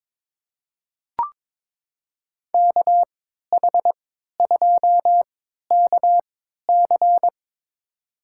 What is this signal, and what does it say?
Morse code, heard as a keyed tone, about 22 wpm, spaced as for 13 wpm, 700 Hz. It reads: KH2KC
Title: USA Call Signs - 5 Characters - 3x Character Spacing - 22wpm